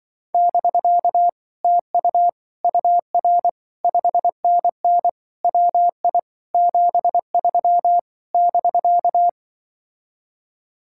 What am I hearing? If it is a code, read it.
Morse code, heard as a keyed tone, about 25 wpm, 700 Hz. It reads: <BK> TU UR 5NN WI 73 <BK>
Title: Speed-Racing Common QSO Phrases - 20wpm